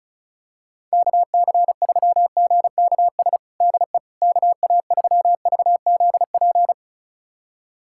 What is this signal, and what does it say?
Morse code, heard as a keyed tone, about 34 wpm, 700 Hz. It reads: KC3GKS DE KA3VZP
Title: USA Callsign DE Callsign - 6 Characters - 35wpm